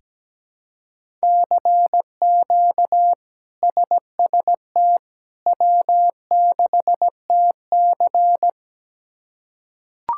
Morse code, heard as a keyed tone, about 17 wpm, 700 Hz. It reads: CQ SST W6TC
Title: K1USN SST QSO Practice - 17wpm